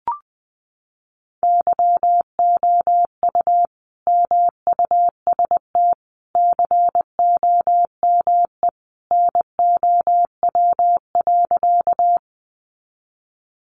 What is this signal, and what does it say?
Morse code, heard as a keyed tone, about 20 wpm, 700 Hz. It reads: YOU MUST COME NOW.
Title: Sentences from Top 200 Words 20wpm